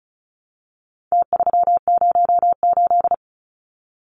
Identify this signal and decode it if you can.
Morse code, heard as a keyed tone, about 34 wpm, 700 Hz. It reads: T308
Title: Numbers - Contest Style - 4-Digit 35wpm